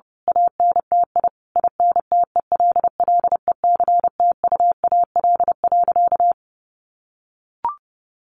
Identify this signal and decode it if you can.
Morse code, heard as a keyed tone, about 30 wpm, 700 Hz. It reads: ANTI INTELLECTUAL.
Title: A Sound Of Thunder 30wpm